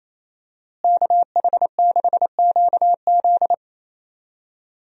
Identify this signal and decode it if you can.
Morse code, heard as a keyed tone, about 28 wpm, 700 Hz. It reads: KH6QZ
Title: USA Call Signs - 5 Characters - Repeated 2x - 28wpm